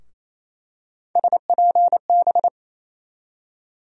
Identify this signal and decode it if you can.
Morse code, heard as a keyed tone, about 28 wpm, 700 Hz. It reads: SPB
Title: USA Call Sign Postfixes - 3 Letters 28wpm